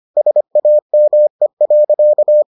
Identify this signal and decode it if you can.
Morse code, heard as a keyed tone, about 25 wpm, 600 Hz. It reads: SAME.